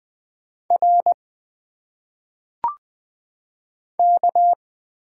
Morse code, heard as a keyed tone, about 20 wpm, 700 Hz. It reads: R K
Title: Single Letters - Phonetic Pronunciation 20wpm